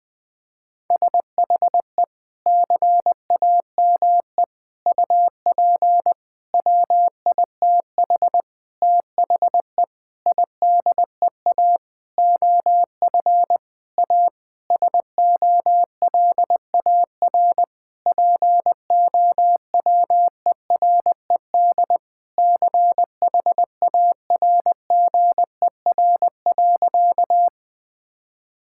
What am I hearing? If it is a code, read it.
Morse code, heard as a keyed tone, about 20 wpm, 700 Hz. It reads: SHE CAME UP WITH THE IDEA OF A SOLAR POWERED CHARGER.
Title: Sentences with 6-Word Phrases - Spoken Beginning - 20wpm